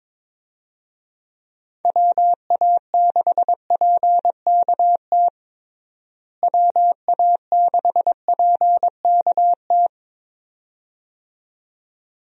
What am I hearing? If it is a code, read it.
Morse code, heard as a keyed tone, about 21 wpm, 700 Hz. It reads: WA6PKT WA6PKT
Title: USA Call Signs - 6 Characters - Repeated 2x - 22wpm